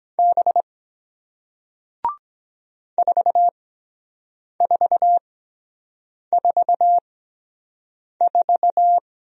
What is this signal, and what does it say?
Morse code, heard as a keyed tone, about 23 wpm, 700 Hz. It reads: B 4 4 4 4